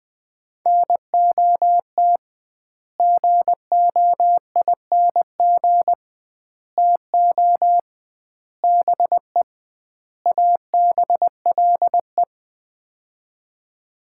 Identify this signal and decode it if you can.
Morse code, heard as a keyed tone, about 21 wpm, 700 Hz. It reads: NOT GOING TO BE ABLE
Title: 2x Word-Spacing - Sets of 5 Words - 20wpm